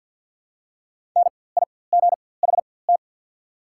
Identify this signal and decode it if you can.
Morse code, heard as a keyed tone, about 48 wpm, spaced as for 23 wpm, 700 Hz. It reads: NIGHT
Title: 4x Character Spacing - Top 300 Words - 50wpm